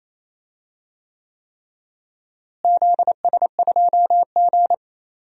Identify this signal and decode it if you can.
Morse code, heard as a keyed tone, about 28 wpm, 700 Hz. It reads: ZS2G
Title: DXCC-100 Call Signs - 4 Characters - Repeated 3x - 28wpm